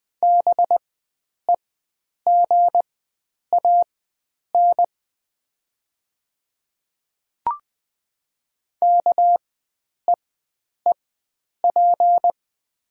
Morse code, heard as a keyed tone, about 20 wpm, spaced as for 9 wpm, 700 Hz. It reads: BEGAN KEEP
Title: 4x Character Spacing - Top 300 Words - 20wpm